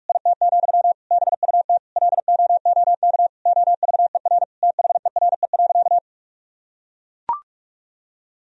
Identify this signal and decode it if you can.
Morse code, heard as a keyed tone, about 44 wpm, 700 Hz. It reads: IT, BUT LOOK OVER THERE.
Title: Sentences from Top 1000 Words - Review Part 2 - 45wpm